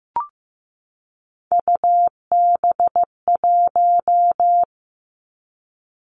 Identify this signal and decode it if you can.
Morse code, heard as a keyed tone, about 15 wpm, 700 Hz. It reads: UB1
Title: Three Challenging Letter-Number Combinations 15wpm